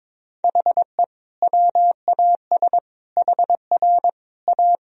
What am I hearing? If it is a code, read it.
Morse code, heard as a keyed tone, about 22 wpm, 700 Hz. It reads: HE WAS HR A
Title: Sentences from Top 500 Words and Abbreviations 22wpm